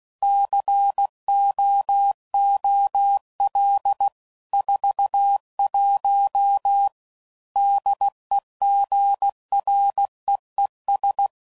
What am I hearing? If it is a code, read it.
Morse code, heard as a keyed tone, about 16 wpm, 800 Hz. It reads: COOL 41 DEGREES
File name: CW Code Groups @15WPM/Weather Sets